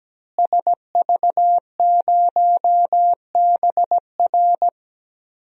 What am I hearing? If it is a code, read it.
Morse code, heard as a keyed tone, about 17 wpm, 700 Hz. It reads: SV0BR
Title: DXCC-100 Call Signs - 5 Characters - Repeated 3x - 17wpm